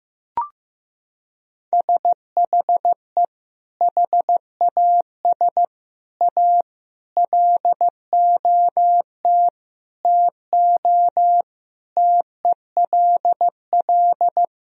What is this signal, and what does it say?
Morse code, heard as a keyed tone, about 15 wpm, 700 Hz. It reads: SHE HAS A LOT TO TELL
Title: Sentences from Top 1000 Words - Review Part 3 - 15wpm